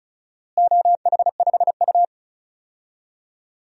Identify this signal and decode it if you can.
Morse code, heard as a keyed tone, about 34 wpm, 700 Hz. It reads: OH5U
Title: DXCC-100 Call Signs - 4 Characters - Repeated 2x - 35wpm